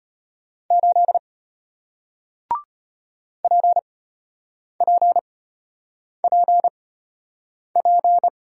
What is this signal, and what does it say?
Morse code, heard as a keyed tone, about 34 wpm, spaced as for 8 wpm, 700 Hz. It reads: 8 PPPP